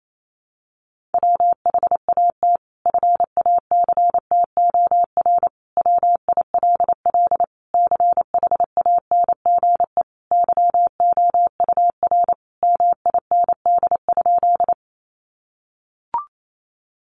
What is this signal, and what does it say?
Morse code, heard as a keyed tone, about 28 wpm, 700 Hz. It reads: WHAT FACTOR WILL CHANGE YOUR MIND?
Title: Sentences from Top 1300 Words 28wpm